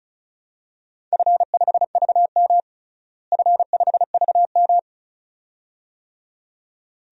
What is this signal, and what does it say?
Morse code, heard as a keyed tone, about 34 wpm, 700 Hz. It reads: F5VM F5VM
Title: DXCC-100 Call Signs - 4 Characters - Repeated 2x - 35wpm